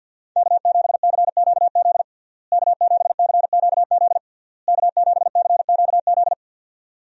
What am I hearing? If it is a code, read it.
Morse code, heard as a keyed tone, about 50 wpm, 700 Hz. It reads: K7CQZ K7CQZ K7CQZ
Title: USA Call Signs - 5 Characters - Repeated 3x - 50wpm